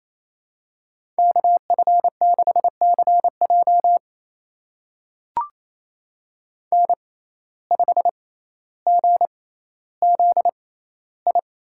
Morse code, heard as a keyed tone, about 28 wpm, spaced as for 14 wpm, 700 Hz. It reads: KF6CJ N5GZI
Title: USA Call Signs - 5 Characters - 6x Character Spacing - 28wpm